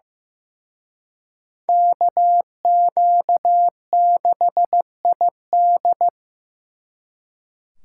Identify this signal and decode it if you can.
Morse code, heard as a keyed tone, about 18 wpm, 700 Hz. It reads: KQ6ID
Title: Speed Racing - USA 2x2 Call Signs 15wpm